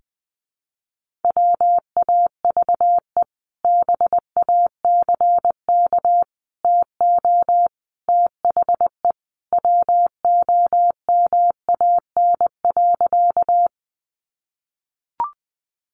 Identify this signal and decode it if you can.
Morse code, heard as a keyed tone, about 20 wpm, 700 Hz. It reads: WAVE BACK TO THE WOMAN.
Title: Sentences from Top 1400 Words 20wpm